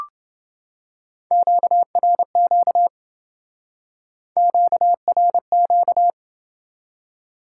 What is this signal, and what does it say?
Morse code, heard as a keyed tone, about 26 wpm, 700 Hz. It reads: QRQ QRQ